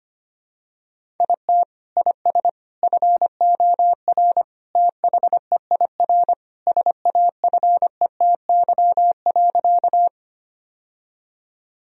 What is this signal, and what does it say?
Morse code, heard as a keyed tone, about 25 wpm, 700 Hz. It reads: IT IS FOR THEIR SAFETY.
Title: Sentences from Top 1000 Words - Review Part 4 - 25wpm